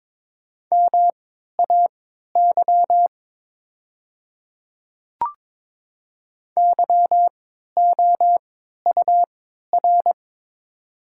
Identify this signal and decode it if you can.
Morse code, heard as a keyed tone, about 21 wpm, spaced as for 12 wpm, 700 Hz. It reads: MAY YOUR
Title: 3x Character Spacing - Top 500 Words - 22wpm